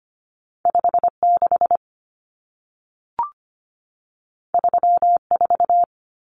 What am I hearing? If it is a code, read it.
Morse code, heard as a keyed tone, about 25 wpm, 700 Hz. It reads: 56 34
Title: Numbers - 2-Digit 25wpm